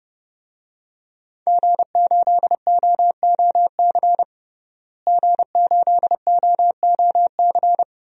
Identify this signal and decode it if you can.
Morse code, heard as a keyed tone, about 30 wpm, 700 Hz. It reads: G8OOC G8OOC
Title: DXCC-100 Call Signs - 5 Characters - Repeated 3x - 30wpm